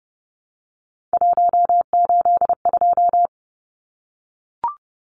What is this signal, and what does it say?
Morse code, heard as a keyed tone, about 30 wpm, 700 Hz. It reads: 182
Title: Numbers - 3-Digit 30wpm